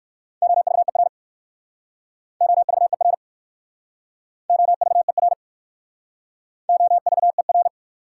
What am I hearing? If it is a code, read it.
Morse code, heard as a keyed tone, about 55 wpm, 690 Hz. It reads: OVER OVER OVER OVER
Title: Speed-Racing Prefixes and Suffixes 45wpm